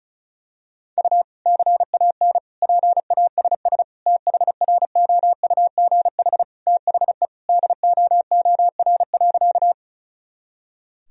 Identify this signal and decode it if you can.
Morse code, heard as a keyed tone, about 36 wpm, 700 Hz. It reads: U CAN PASS THROUGH THE DOOR.
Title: Sentences from Top 400 Words and Abbreviations 35wpm